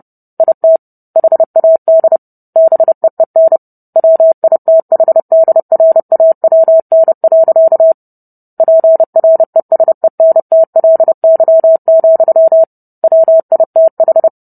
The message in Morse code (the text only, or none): IT HAD BEEN WITHDRAWN. PRESENTLY, WITH